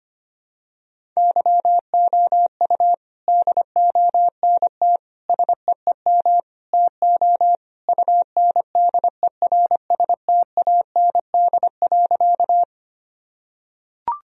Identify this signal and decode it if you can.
Morse code, heard as a keyed tone, about 25 wpm, 700 Hz. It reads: YOU DONT SEEM TO UNDERSTAND.
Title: Sentences from Top 1000 Words - Review Part 1 - 25wpm